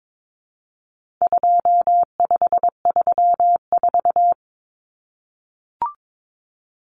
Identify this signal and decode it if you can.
Morse code, heard as a keyed tone, about 22 wpm, 700 Hz. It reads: 2534
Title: Numbers - 4-Digit 22wpm